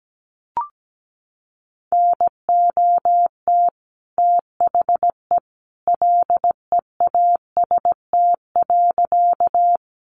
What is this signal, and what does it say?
Morse code, heard as a keyed tone, about 17 wpm, 700 Hz. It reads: NOT THE LEAST.
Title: Sentences from Top 700 Words 17wpm